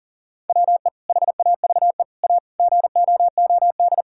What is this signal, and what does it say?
Morse code, heard as a keyed tone, about 40 wpm, 700 Hz. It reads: WE HAVE A GOOD
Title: Sentences from Top 1000 Words - Review Part 5 - 40wpm